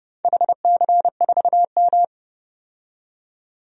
Morse code, heard as a keyed tone, about 28 wpm, 700 Hz. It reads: HC4M
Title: DXCC-100 Call Signs - 4 Characters - 30wpm